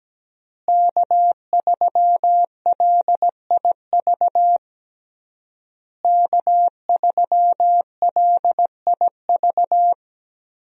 Morse code, heard as a keyed tone, about 17 wpm, 700 Hz. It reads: K3LIV K3LIV
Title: USA Call Signs - 5 Characters - Repeated 3x - 17wpm